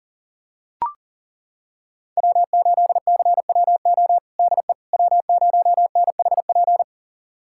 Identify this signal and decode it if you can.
Morse code, heard as a keyed tone, about 40 wpm, 700 Hz. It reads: W8CWO DE W0NHP
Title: USA Callsign DE Callsign - 5 Characters - 40wpm